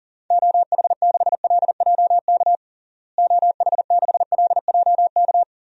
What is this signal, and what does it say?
Morse code, heard as a keyed tone, about 37 wpm, 700 Hz. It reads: OH6LJK OH6LJK
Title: DXCC-100 Call Signs - 6 Characters - Repeated 2x - 40wpm